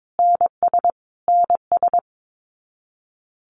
Morse code, heard as a keyed tone, about 22 wpm, 700 Hz. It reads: NS NS
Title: Canadian Provinces and Territories - Repeated 3x - 22wpm